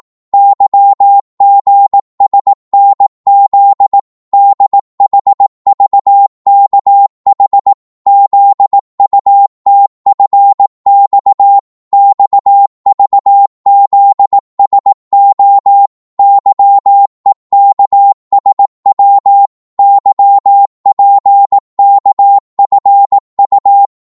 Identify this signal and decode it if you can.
Morse code, heard as a keyed tone, about 18 wpm, 800 Hz. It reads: YGSNZ DHVKH ZUTFX XVZSO YEKSW YPKFU
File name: Morse / ltr-18wpm